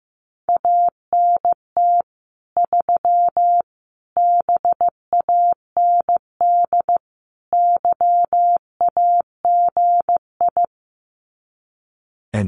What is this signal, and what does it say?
Morse code, heard as a keyed tone, about 15 wpm, 700 Hz. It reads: ANT 3 BAND YAGI
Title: Common QSO Phrases 15wpm